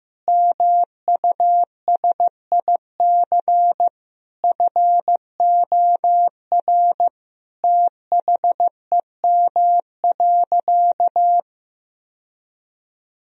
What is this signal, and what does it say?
Morse code, heard as a keyed tone, about 15 wpm, 700 Hz. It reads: MUSIC FOR THEM.
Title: Sentences from Top 1000 Words - Review Part 1 - 15wpm